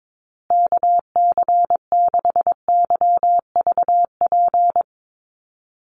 Morse code, heard as a keyed tone, about 22 wpm, 700 Hz. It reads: KC6YVP
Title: USA Call Signs Repeated 3x 22wpm